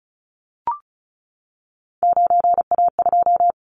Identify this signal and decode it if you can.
Morse code, heard as a keyed tone, about 36 wpm, 700 Hz. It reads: 9A2
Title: Callsign Prefixes for Top 50 DXCC - 35wpm